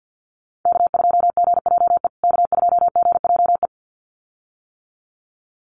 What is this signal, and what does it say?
Morse code, heard as a keyed tone, about 50 wpm, 700 Hz. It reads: K2GJE K2GJE
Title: USA Call Signs Repeated 2x 50wpm